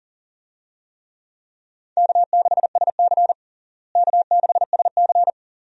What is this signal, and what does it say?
Morse code, heard as a keyed tone, about 40 wpm, 700 Hz. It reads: K6SC K6SC
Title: USA Call Signs - 4 Characters - Repeated 2x - 40wpm